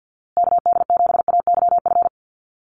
Morse code, heard as a keyed tone, about 50 wpm, 700 Hz. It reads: KD7AYF
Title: USA Call Signs Repeated 3x 50wpm